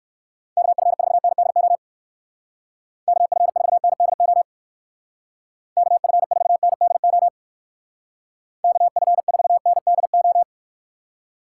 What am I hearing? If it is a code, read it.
Morse code, heard as a keyed tone, about 56 wpm, 700 Hz. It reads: KF4NDO KF4NDO KF4NDO KF4NDO
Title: Speed Racing - USA 2x3 Call Signs 45wpm